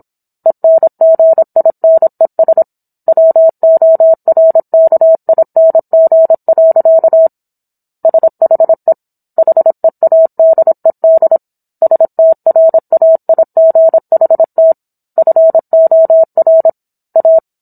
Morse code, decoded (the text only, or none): ENGINES WORKING. SHE HEADED STRAIGHT FOR A